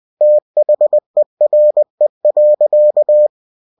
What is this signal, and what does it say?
Morse code, heard as a keyed tone, about 20 wpm, 600 Hz. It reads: THERE.
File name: cw-books / winnie-the-pooh / winnie-20wpm-page-0101